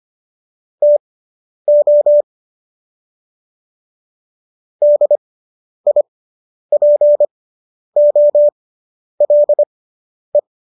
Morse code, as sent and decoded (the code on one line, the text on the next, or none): - --- / -.. .. .--. --- .-.. .
TO DIPOLE